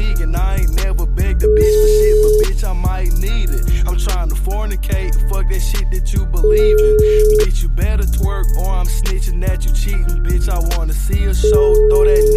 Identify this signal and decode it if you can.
Ringback tone (Europe, 425 Hz).